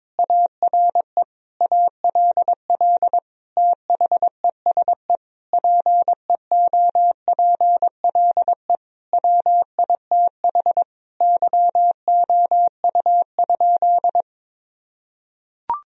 Morse code, heard as a keyed tone, about 22 wpm, 700 Hz. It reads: ARE ALL THESE PEOPLE WITH YOU?
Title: Sentences from Top 1000 Words - Review Part 1 - 22wpm